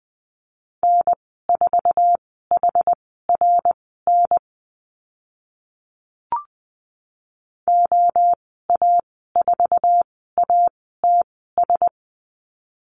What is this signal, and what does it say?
Morse code, heard as a keyed tone, about 20 wpm, spaced as for 14 wpm, 700 Hz. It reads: N4HRN OA4ATS
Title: 088 Lesson 15g - Number 5 - Call Signs 20wpm